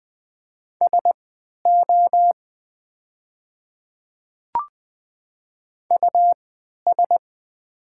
Morse code, heard as a keyed tone, about 20 wpm, spaced as for 11 wpm, 690 Hz. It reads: SO US
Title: 3x Character Spacing - Top 500 Words - 20wpm